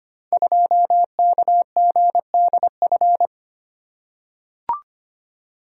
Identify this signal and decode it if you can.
Morse code, heard as a keyed tone, about 25 wpm, 700 Hz. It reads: 2KGDF
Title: Code Groups - 5 Letter:Numbers - 25wpm